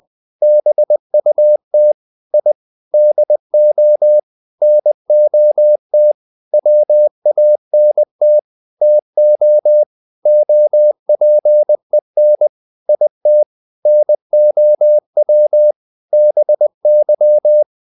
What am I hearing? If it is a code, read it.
Morse code, heard as a keyed tone, about 20 wpm, 600 Hz. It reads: BUT I DO NOT WANT TO OPEN IT NOW BY
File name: the old man and the sea / the old man and the sea part 1 20wpm